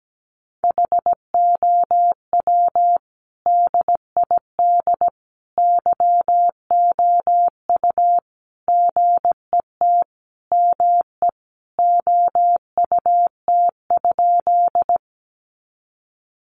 Morse code, heard as a keyed tone, about 17 wpm, 700 Hz. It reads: HOW DID YOU GET ME OUT?
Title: Sentences from Top 200 Words 17wpm